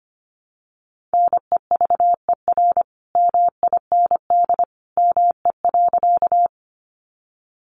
Morse code, heard as a keyed tone, about 25 wpm, 700 Hz. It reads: NEVER MIND ME.
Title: Sentences from Top 500 Words 25wpm